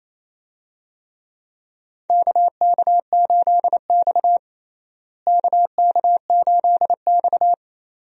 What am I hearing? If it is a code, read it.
Morse code, heard as a keyed tone, about 28 wpm, 700 Hz. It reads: KK8X KK8X
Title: USA Call Signs - 4 Characters - Repeated 2x - 28wpm